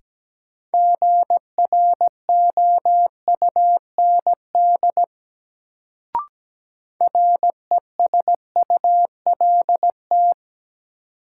Morse code, heard as a keyed tone, about 17 wpm, 700 Hz. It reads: GROUND RESULT
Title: Sets of 3 Words - New Word Drill - 17wpm